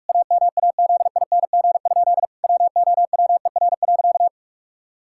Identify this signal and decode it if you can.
Morse code, heard as a keyed tone, about 41 wpm, 700 Hz. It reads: AMAZING? WOWER.
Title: Pangram Challenge v2 - 45wpm